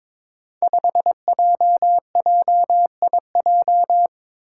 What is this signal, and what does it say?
Morse code, heard as a keyed tone, about 22 wpm, 700 Hz. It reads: 5JJIJ
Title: Five Challenging Letter-Number Combinations 22wpm